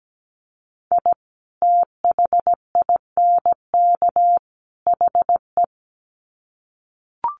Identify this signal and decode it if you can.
Morse code, heard as a keyed tone, about 17 wpm, 700 Hz. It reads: I THINK HE
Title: Sets of 3 Words 17wpm